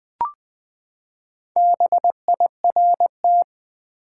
Morse code, heard as a keyed tone, about 20 wpm, 700 Hz. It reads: BIRT
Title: Build A Word - 4 Letters At A Time - 8-Letter Words - 20wpm